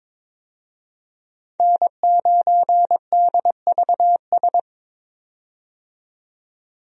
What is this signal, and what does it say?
Morse code, heard as a keyed tone, about 22 wpm, 700 Hz. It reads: N9DVS
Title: USA Call Signs - 5 Characters - 22wpm